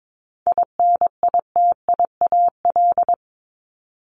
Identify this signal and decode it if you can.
Morse code, heard as a keyed tone, about 22 wpm, 700 Hz. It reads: INITIAL
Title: Top 1501-1750 Words 22wpm